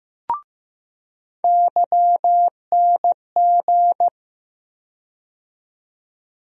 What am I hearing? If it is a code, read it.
Morse code, heard as a keyed tone, about 15 wpm, 700 Hz. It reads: YNG
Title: USA Call Sign Postfixes - 3 Letters 15wpm